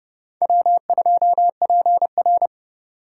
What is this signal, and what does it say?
Morse code, heard as a keyed tone, about 30 wpm, 700 Hz. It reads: W2PR
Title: Speed Racing - USA 1x2 Call Signs 25wpm